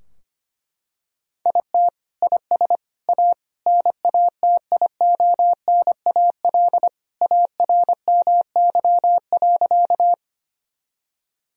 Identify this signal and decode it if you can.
Morse code, heard as a keyed tone, about 25 wpm, 700 Hz. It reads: IT IS A NATIONAL ARMY.
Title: Sentences from Top 1000 Words - Review Part 4 - 25wpm